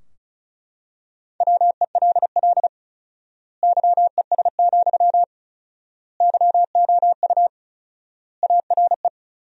Morse code, heard as a keyed tone, about 36 wpm, 700 Hz. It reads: WELL YES, YOU ARE
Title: 4x Word-Spacing - Sentences from Top 500 Words - 35wpm